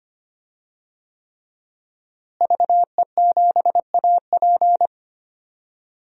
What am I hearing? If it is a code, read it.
Morse code, heard as a keyed tone, about 25 wpm, 700 Hz. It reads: VE7AP
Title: DXCC-100 Call Signs - 5 Characters - Repeated 3x - 25wpm